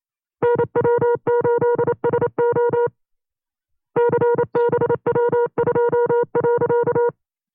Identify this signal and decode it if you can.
Morse code, heard as a keyed tone, about 28 wpm, 480 Hz. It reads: NW8SO CBW2.
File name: FreeFiles / CW-practice / 25WPM-side2